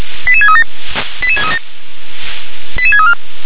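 Five-tone selective call (ZVEI), heard as quick runs of five tones, tones of about 70 ms. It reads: 88637 88637 88634
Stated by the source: ZVEI five-tone call 88631 received on SDR.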